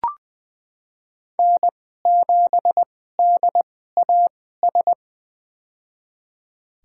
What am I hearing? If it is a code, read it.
Morse code, heard as a keyed tone, about 20 wpm, spaced as for 14 wpm, 700 Hz. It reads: N7DAS
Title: USA Call Signs - 5 Characters - 2x Character Spacing - 20wpm